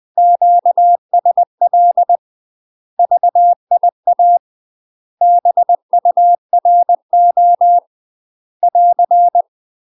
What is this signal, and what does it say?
Morse code, heard as a keyed tone, about 20 wpm, 700 Hz. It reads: QSL VIA BURO <AR>
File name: UK MASTER QSOs TRAINING / 20WPM / UK MASTER QSOs 0063